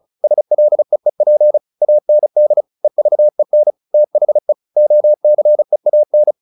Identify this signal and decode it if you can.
Morse code, heard as a keyed tone, about 34 wpm, 600 Hz. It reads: SLEEP AND EVEN THE OCEAN
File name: the old man and the sea 2 35wpm